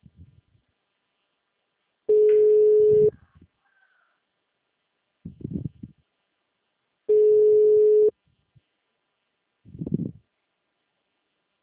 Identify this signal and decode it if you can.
Ringback tone (Europe, 425 Hz), with speech.